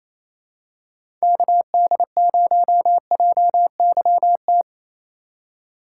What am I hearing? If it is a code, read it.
Morse code, heard as a keyed tone, about 28 wpm, 700 Hz. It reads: KD0JYT
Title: USA Call Signs - 2x Character Spacing - 28wpm